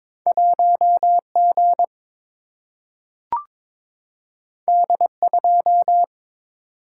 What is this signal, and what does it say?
Morse code, heard as a keyed tone, about 22 wpm, 700 Hz. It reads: 1 G D 2
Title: Single-Letter Single-Number 22wpm